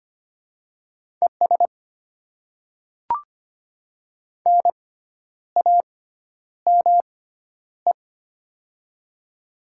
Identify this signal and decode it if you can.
Morse code, heard as a keyed tone, about 25 wpm, spaced as for 10 wpm, 700 Hz. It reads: ES NAME
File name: Top 20 QSO Elements - 6x Character Spacing - 25wpm